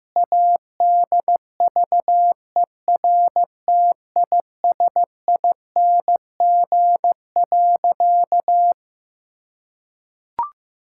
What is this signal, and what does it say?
Morse code, heard as a keyed tone, about 15 wpm, 700 Hz. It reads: ADVERTISING.
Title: Sentences from Top 1000 Words - Review Part 6 - 15wpm